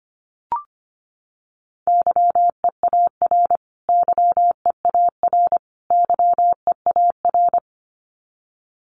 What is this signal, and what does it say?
Morse code, heard as a keyed tone, about 25 wpm, 700 Hz. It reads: YEAR YEAR YEAR
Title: Top 200 Words Repeated 25wpm